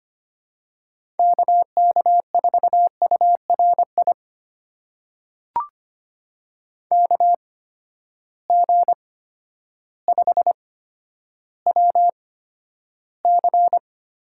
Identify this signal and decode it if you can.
Morse code, heard as a keyed tone, about 25 wpm, spaced as for 11 wpm, 700 Hz. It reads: KK4URI KG5WC
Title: USA Call Signs - 6 Characters - 8x Character Spacing - 25wpm